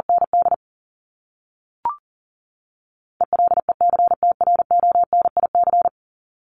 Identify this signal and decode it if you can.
Morse code, heard as a keyed tone, about 40 wpm, 700 Hz. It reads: ND ELECTRONIC
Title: Top 1001-1250 Words 40wpm